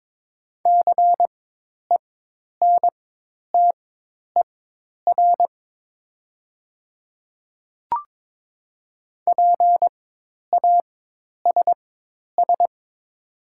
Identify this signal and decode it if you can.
Morse code, heard as a keyed tone, about 22 wpm, spaced as for 10 wpm, 700 Hz. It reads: CENTER PASS
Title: 4x Character Spacing - Top 500 Words - 22wpm